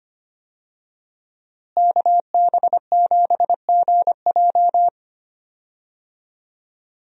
Morse code, heard as a keyed tone, about 25 wpm, 700 Hz. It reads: KB7GJ
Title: USA Call Signs - 5 Characters - Repeated 3x - 25wpm